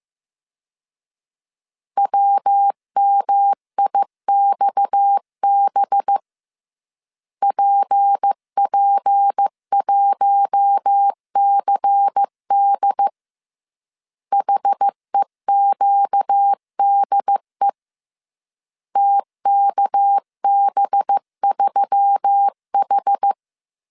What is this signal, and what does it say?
Morse code, heard as a keyed tone, about 15 wpm, 800 Hz. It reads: WMIXB PP1CD HEQDE TKB3H